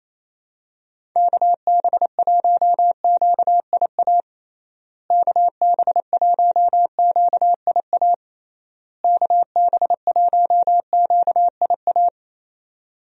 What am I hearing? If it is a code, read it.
Morse code, heard as a keyed tone, about 28 wpm, 700 Hz. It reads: KB1QIA KB1QIA KB1QIA
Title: USA Call Signs - 6 Characters - Repeated 3x - 28wpm